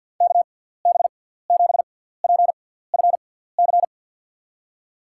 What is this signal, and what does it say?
Morse code, heard as a keyed tone, about 52 wpm, spaced as for 17 wpm, 700 Hz. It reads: KB7PFC
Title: USA Call Signs - 6 Characters - 6x Character Spacing - 50wpm